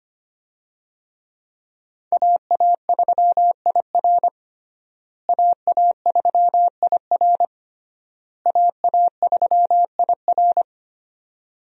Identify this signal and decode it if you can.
Morse code, heard as a keyed tone, about 25 wpm, 700 Hz. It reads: AA3IR AA3IR AA3IR
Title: USA Call Signs - 5 Characters - Repeated 3x - 25wpm